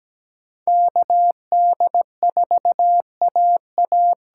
Morse code, heard as a keyed tone, about 17 wpm, 700 Hz. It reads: KD4AA
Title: USA Call Signs - 6 Characters - 8x Character Spacing - 17wpm